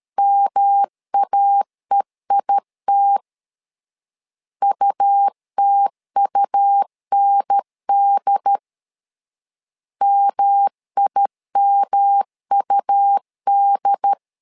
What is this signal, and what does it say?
Morse code, heard as a keyed tone, about 13 wpm, 800 Hz. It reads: MAEIT UTUND MIMUD
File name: LEZIONI / 3 MWRO / 3 M 12wpm 1541